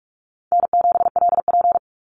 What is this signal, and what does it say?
Morse code, heard as a keyed tone, about 46 wpm, 700 Hz. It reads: N7LP
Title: NA SOTA Activator Callsigns 45wpm